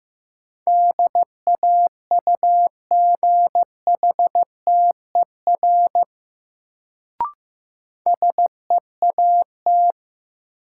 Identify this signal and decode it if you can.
Morse code, heard as a keyed tone, about 15 wpm, 700 Hz. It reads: DAUGHTER SEAT
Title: Sentences from Top 1500 Words - New Word Drill Alt - 15wpm